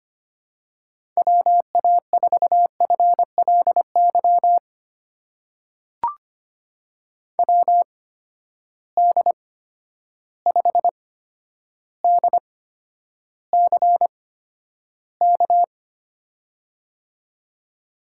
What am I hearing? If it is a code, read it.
Morse code, heard as a keyed tone, about 25 wpm, spaced as for 11 wpm, 700 Hz. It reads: WA4FLY WD5DCK
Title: USA Call Signs - 6 Characters - 8x Character Spacing - 25wpm